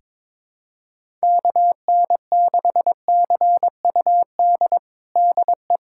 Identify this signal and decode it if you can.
Morse code, heard as a keyed tone, about 22 wpm, 700 Hz. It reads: KN6CUD DE
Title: USA Callsign DE Callsign - 6 Characters - 22wpm